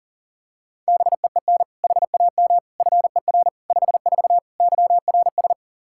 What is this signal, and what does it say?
Morse code, heard as a keyed tone, about 40 wpm, 700 Hz. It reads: BEEN HAM FER 54 YRS
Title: K1USN SST QSO Practice 40wpm